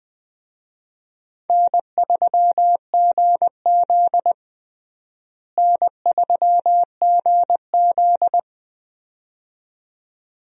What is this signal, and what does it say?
Morse code, heard as a keyed tone, about 20 wpm, 700 Hz. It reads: N3GZ N3GZ
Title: USA Call Signs - 4 Characters - Repeated 2x - 20wpm